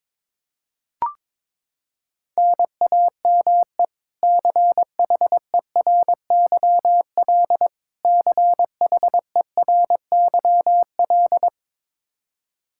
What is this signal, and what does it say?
Morse code, heard as a keyed tone, about 22 wpm, 700 Hz. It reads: NAME CHERYL CHERYL
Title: K1USN SST QSO Practice 22wpm